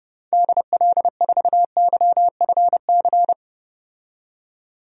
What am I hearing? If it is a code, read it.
Morse code, heard as a keyed tone, about 30 wpm, 700 Hz. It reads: DL4YFC
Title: DXCC-100 Call Signs - 6 Characters - 30wpm